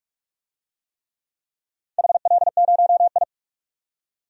Morse code, heard as a keyed tone, about 41 wpm, 700 Hz. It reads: HL0I
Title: DXCC-100 Call Signs - 4 Characters - 45wpm